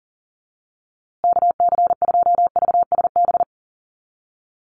Morse code, heard as a keyed tone, about 40 wpm, 700 Hz. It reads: KC2VSB
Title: USA Call Signs Repeated 3x 40wpm